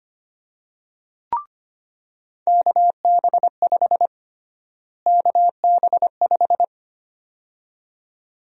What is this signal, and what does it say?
Morse code, heard as a keyed tone, about 25 wpm, 700 Hz. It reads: KB5 KB5
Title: USA Call Sign Prefixes - Repeated 2x - 25wpm